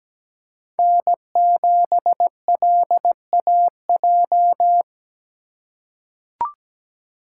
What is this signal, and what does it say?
Morse code, heard as a keyed tone, about 17 wpm, 700 Hz. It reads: N7LAJ
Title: USA Call Signs - 5 Characters - 4x Character Spacing - 17wpm